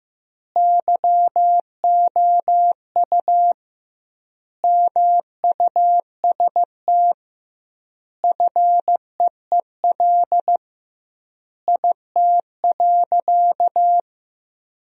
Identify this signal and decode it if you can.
Morse code, heard as a keyed tone, about 15 wpm, 700 Hz. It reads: YOU MUST FEEL IT.
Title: 2x Word-Spacing - Sentences from Top 400 Words - 15wpm